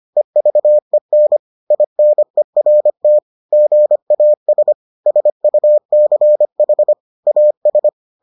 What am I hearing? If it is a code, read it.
Morse code, heard as a keyed tone, about 25 wpm, 600 Hz. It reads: EVEN INERT GAS SUCH AS